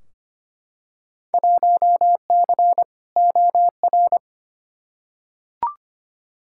Morse code, heard as a keyed tone, about 25 wpm, 700 Hz. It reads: 1C OR
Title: Field Day Exchanges - 25wpm